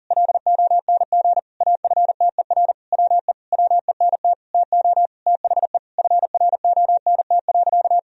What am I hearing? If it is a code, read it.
Morse code, heard as a keyed tone, about 37 wpm, 700 Hz. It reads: LONG AFTER WE WENT TO THE FRONT.
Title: Sentences from Top 1000 Words - Review Part 2 - 40wpm